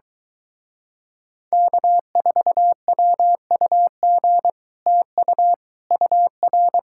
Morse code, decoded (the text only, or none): K4WUG TU UR